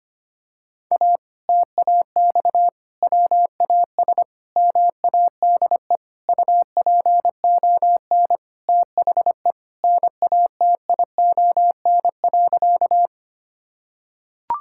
Morse code, heard as a keyed tone, about 25 wpm, 700 Hz. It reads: A TAX WAS MADE UPON THE NATION.
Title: Sentences from Top 1000 Words - Review Part 6 - 25wpm